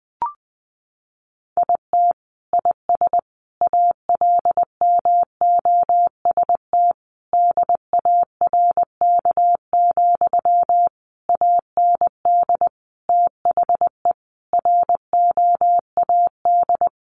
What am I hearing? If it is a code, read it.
Morse code, heard as a keyed tone, about 20 wpm, 700 Hz. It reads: IT IS ALMOST DARK, AND THE ROAD
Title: Sentences from Top 1200 Words 20wpm